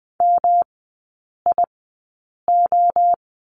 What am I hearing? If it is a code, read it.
Morse code, heard as a keyed tone, about 20 wpm, spaced as for 8 wpm, 700 Hz. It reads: MIO
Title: 198 Lesson 32e - Letter k - Three Characters 20wpm